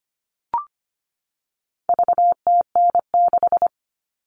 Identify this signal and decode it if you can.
Morse code, heard as a keyed tone, about 25 wpm, 700 Hz. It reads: VTN6
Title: All Cut Numbers - 4-Digit 25wpm